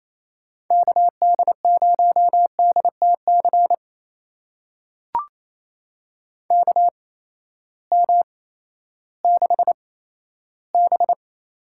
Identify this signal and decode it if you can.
Morse code, heard as a keyed tone, about 28 wpm, spaced as for 14 wpm, 700 Hz. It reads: KD0DTC KM6B
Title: USA Call Signs - 6 Characters - 8x Character Spacing - 28wpm